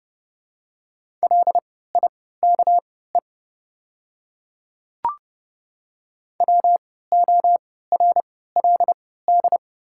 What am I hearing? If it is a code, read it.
Morse code, heard as a keyed tone, about 30 wpm, spaced as for 17 wpm, 700 Hz. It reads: LIKE WORLD
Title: 3x Character Spacing - Top 300 Words - 30wpm